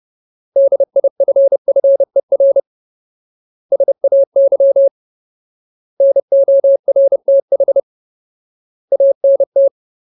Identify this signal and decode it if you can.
Morse code, heard as a keyed tone, about 30 wpm, 550 Hz. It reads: DIFFER SAY NORTH ANT
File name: Ultimate Word List / 30WPM Ultimate